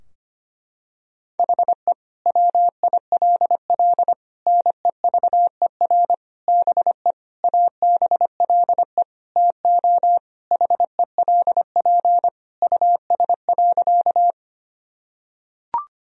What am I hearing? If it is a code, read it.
Morse code, heard as a keyed tone, about 25 wpm, 700 Hz. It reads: HE WILL NEVER BE ABLE TO HELP US.
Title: Sentences from Top 1000 Words - Review Part 2 - 25wpm